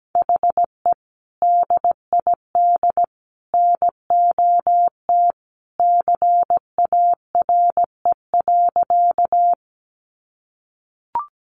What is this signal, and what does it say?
Morse code, heard as a keyed tone, about 17 wpm, 700 Hz. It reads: HE DID NOT CARE.
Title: Sentences from Top 600 Words 17wpm